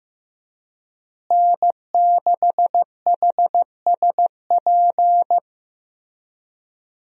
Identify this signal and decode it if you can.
Morse code, heard as a keyed tone, about 15 wpm, 700 Hz. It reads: N6HSP
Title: USA Call Signs - 5 Characters - Repeated 2x - 15wpm